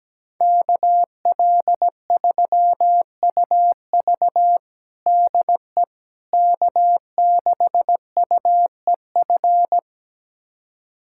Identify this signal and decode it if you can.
Morse code, heard as a keyed tone, about 17 wpm, 700 Hz. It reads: KL3UV DE K6UEF
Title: USA Callsign DE Callsign - 5 Characters - 17wpm